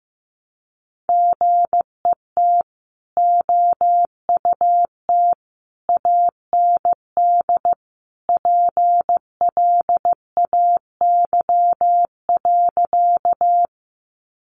Morse code, heard as a keyed tone, about 15 wpm, 700 Hz. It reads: GET OUT AND PLAY.
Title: Sentences from Top 200 Words 15wpm